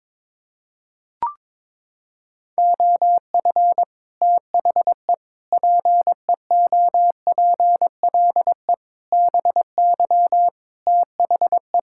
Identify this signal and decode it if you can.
Morse code, heard as a keyed tone, about 21 wpm, 700 Hz. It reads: OF THE PEOPLE BY THE
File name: Sets of 9 Words - 22wpm